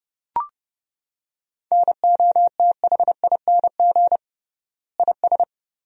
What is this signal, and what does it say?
Morse code, heard as a keyed tone, about 28 wpm, 700 Hz. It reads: NOTHING IS
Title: 3x Word-Spacing - Sentences from Top 500 Words - 30wpm